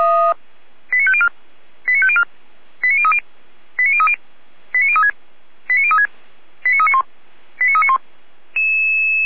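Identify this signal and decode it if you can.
Five-tone selective call (ZVEI), heard as quick runs of five tones, tones of about 70 ms. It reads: 89504 89504 89030 89030 89036 89036 89391 89391
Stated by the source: ZVEI five-tone selective call received by SDR.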